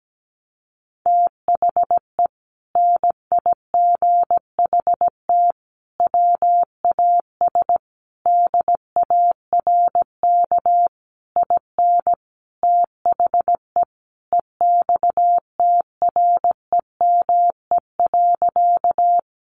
Morse code, heard as a keyed tone, about 17 wpm, 700 Hz. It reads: THE NIGHT WAS DARK IN THE EXTREME.
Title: Sentences from Top 1300 Words 17wpm